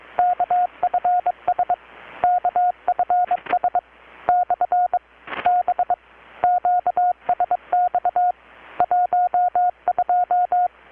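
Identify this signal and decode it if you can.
Morse code, heard as a keyed tone, about 21 wpm, 720 Hz. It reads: KFS KFS / B QSX 12
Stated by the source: Morse from KPH's Night of Nights XX maritime radiotelegraphy event, received in Chiapas, Mexico about 3,800 km away on a Kenwood R-600 with a grounded 10 m long-wire antenna.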